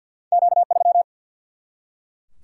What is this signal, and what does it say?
Morse code, heard as a keyed tone, about 50 wpm, 700 Hz. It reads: Q 3
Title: Single-Letter Single-Number 50wpm